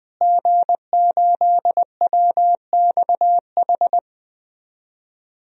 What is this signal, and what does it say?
Morse code, heard as a keyed tone, about 20 wpm, 700 Hz. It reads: G8WXH
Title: DXCC-100 Call Signs - 5 Characters - Repeated 2x - 20wpm